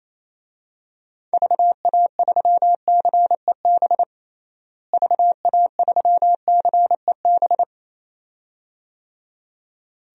Morse code, heard as a keyed tone, about 28 wpm, 700 Hz. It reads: VA3CEB VA3CEB
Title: DXCC-100 Call Signs - 6 Characters - Repeated 2x - 28wpm